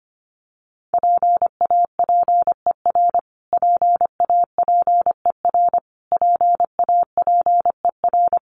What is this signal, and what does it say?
Morse code, heard as a keyed tone, about 25 wpm, 700 Hz. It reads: PAPER PAPER PAPER
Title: Top 500 Words Repeated 25wpm